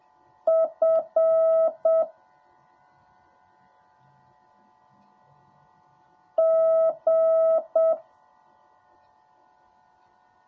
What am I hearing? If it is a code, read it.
Morse code, heard as a keyed tone, about 7 wpm, 650 Hz. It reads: F G
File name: jh8cbh / 2021cw / cw10a